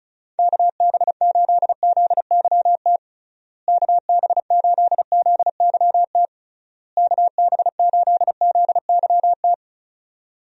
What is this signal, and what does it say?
Morse code, heard as a keyed tone, about 36 wpm, 700 Hz. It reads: KB8ZYT KB8ZYT KB8ZYT
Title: USA Call Signs - 6 Characters - Repeated 3x - 35wpm